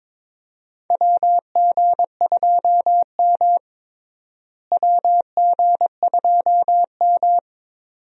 Morse code, heard as a keyed tone, about 22 wpm, 700 Hz. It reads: WG2M WG2M
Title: USA Call Signs - 4 Characters - Repeated 2x - 22wpm